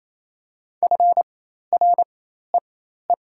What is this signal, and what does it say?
Morse code, heard as a keyed tone, about 28 wpm, spaced as for 13 wpm, 700 Hz. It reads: FREE
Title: 4x Character Spacing - Top 500 Words - 28wpm